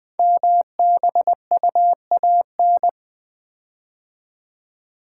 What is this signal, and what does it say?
Morse code, heard as a keyed tone, about 20 wpm, 700 Hz. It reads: MBUAN
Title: Code Groups - 5 Letters - 20wpm